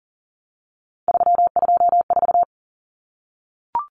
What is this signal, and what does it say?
Morse code, heard as a keyed tone, about 40 wpm, 700 Hz. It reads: 324
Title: Numbers - 3-Digit 40wpm